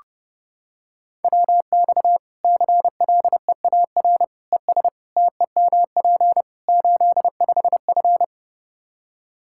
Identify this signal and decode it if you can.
Morse code, heard as a keyed tone, about 30 wpm, 700 Hz. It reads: WX CLEAR ES TEMP 85F
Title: Practice QSO - Exchange 2 - Info - 30wpm